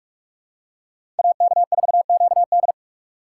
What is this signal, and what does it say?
Morse code, heard as a keyed tone, about 45 wpm, 700 Hz. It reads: AK4QD
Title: USA Call Signs - 5 Characters - 6x Character Spacing - 45wpm